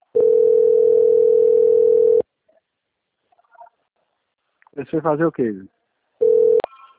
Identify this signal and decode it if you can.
Ringback tone (North America), 440 + 480 Hz, with speech.
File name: out- - - - - .51540